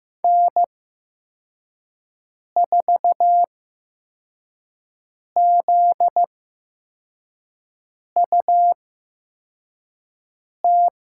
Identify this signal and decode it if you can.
Morse code, heard as a keyed tone, about 15 wpm, spaced as for 4 wpm, 700 Hz. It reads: N4ZUT
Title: USA Call Signs - 5 Characters - 8x Character Spacing - 15wpm